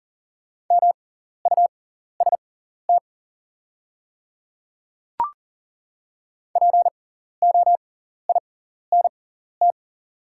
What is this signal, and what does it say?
Morse code, heard as a keyed tone, about 40 wpm, spaced as for 14 wpm, 700 Hz. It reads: MUST POINT
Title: 6x Character Spacing - Top 200 Words - 40wpm